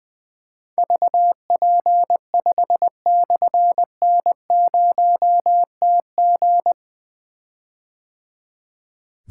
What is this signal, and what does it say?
Morse code, heard as a keyed tone, about 20 wpm, 700 Hz. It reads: VP5/N0TG
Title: Difficult Call Signs 20wpm